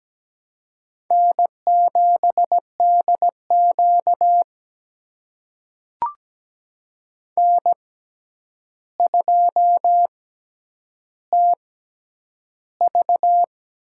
Morse code, heard as a keyed tone, about 17 wpm, spaced as for 9 wpm, 700 Hz. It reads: N7DQ N2TV
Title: USA Call Signs - 4 Characters - 6x Character Spacing - 17wpm